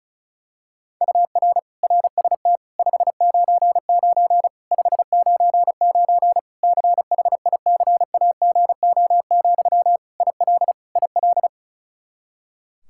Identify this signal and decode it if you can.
Morse code, heard as a keyed tone, about 34 wpm, 700 Hz. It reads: UR RST 599 599 CHICAGO, IL IL
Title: Practice QSO - Exchange 1 - Beginning - 35wpm